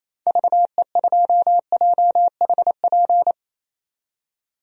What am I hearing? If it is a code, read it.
Morse code, heard as a keyed tone, about 27 wpm, 700 Hz. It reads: VE2JHP
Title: DXCC-100 Call Signs - 6 Characters - 28wpm